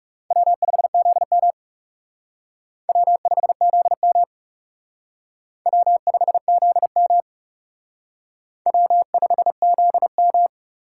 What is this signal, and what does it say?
Morse code, heard as a keyed tone, about 39 wpm, 700 Hz. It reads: W5ZM W5ZM W5ZM W5ZM